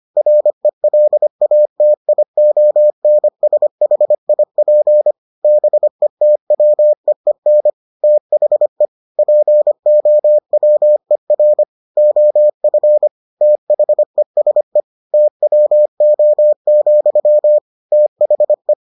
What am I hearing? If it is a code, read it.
Morse code, heard as a keyed tone, about 25 wpm, 600 Hz. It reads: RELATIONSHIP BETWEEN THE POWER OF THESE TWO, THE